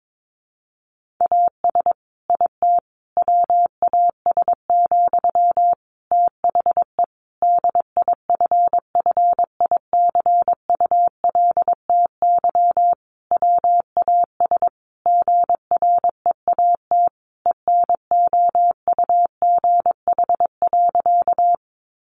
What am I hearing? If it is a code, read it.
Morse code, heard as a keyed tone, about 22 wpm, 700 Hz. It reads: AS IT WAS, THE DIFFICULTY WAS GREAT ENOUGH.
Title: Sentences from Top 1500 Words 22wpm